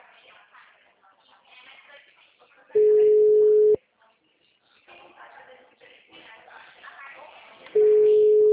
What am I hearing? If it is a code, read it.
Ringback tone (Europe, 425 Hz), with speech.